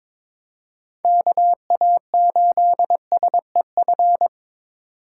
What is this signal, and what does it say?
Morse code, heard as a keyed tone, about 22 wpm, 700 Hz. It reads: KA8SEF
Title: USA Call Signs - 6 Characters - 4x Character Spacing - 22wpm